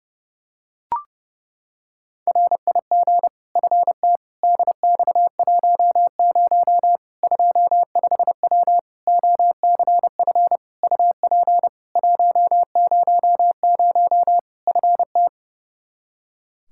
Morse code, heard as a keyed tone, about 30 wpm, 700 Hz. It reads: RIG FT DX10 25W OCF UP 100 FT
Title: Practice QSO - Exchange 2 - Info - 30wpm